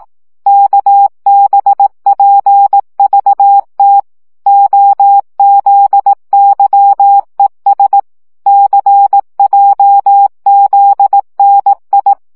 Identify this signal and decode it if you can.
Morse code, heard as a keyed tone, about 18 wpm, 800 Hz. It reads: KBPVT OZYES CJZNI
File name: morze vizsga 2011-1 15